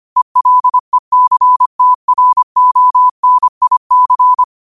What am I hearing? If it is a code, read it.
Morse code, heard as a keyed tone, about 25 wpm, 1000 Hz. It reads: ELECTRONIC
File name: qotdfiles / 25wpm